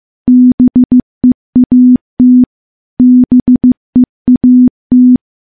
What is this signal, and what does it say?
Morse code, heard as a keyed tone, about 15 wpm, 250 Hz. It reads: BEAT BEAT
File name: morse-training-15WPM